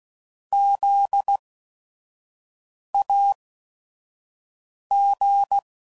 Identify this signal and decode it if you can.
Morse code, heard as a keyed tone, about 16 wpm, spaced as for 5 wpm, 800 Hz. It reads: ZAG
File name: Prefixes / By First Letter / Zs